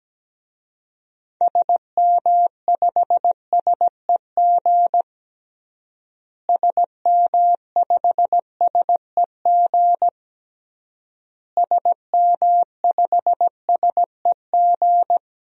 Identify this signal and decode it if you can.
Morse code, heard as a keyed tone, about 17 wpm, 700 Hz. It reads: SM5SEG SM5SEG SM5SEG
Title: DXCC-100 Call Signs - 6 Characters - Repeated 3x - 17wpm